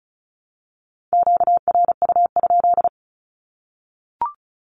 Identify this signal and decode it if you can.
Morse code, heard as a keyed tone, about 34 wpm, 700 Hz. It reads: QRU?
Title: Q-Codes Repeated 3x 35wpm